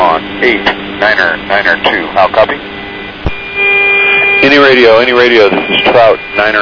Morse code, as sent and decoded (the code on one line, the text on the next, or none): none